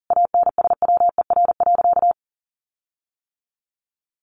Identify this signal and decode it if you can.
Morse code, heard as a keyed tone, about 40 wpm, 700 Hz. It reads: ANSWER.
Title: Sentences from Top 1500 Words 40wpm